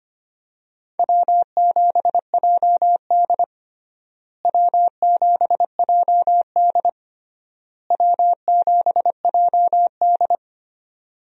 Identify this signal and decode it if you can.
Morse code, heard as a keyed tone, about 25 wpm, 700 Hz. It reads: W7JD W7JD W7JD
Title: USA Call Signs - 4 Characters - Repeated 3x - 25wpm